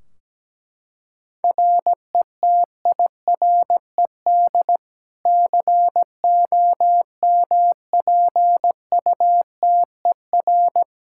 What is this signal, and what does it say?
Morse code, heard as a keyed tone, about 17 wpm, 700 Hz. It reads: RETIRED COMPUTER
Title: Practice QSO - Exchange 3 - More Info - 17wpm